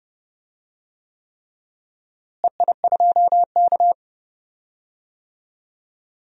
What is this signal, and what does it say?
Morse code, heard as a keyed tone, about 30 wpm, 700 Hz. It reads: EI2K
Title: DXCC-100 Call Signs - 4 Characters - Repeated 3x - 30wpm